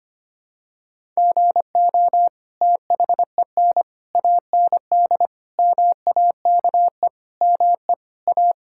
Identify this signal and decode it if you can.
Morse code, heard as a keyed tone, about 25 wpm, 700 Hz. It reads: GO THEN AND MAKE ME A
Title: Sentences from Top 1000 Words - Review Part 5 - 25wpm